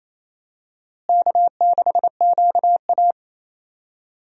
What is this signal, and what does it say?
Morse code, heard as a keyed tone, about 28 wpm, 700 Hz. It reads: K6QA
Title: USA Call Signs - 4 Characters - Repeated 2x - 28wpm